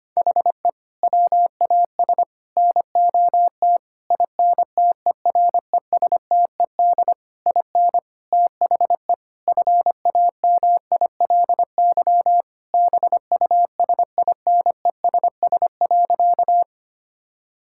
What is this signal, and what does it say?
Morse code, heard as a keyed tone, about 25 wpm, 700 Hz. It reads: HE WAS NOT INTERESTED IN THE FAMILY BUSINESS.
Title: Sentences from Top 1000 Words - Review Part 6 - 25wpm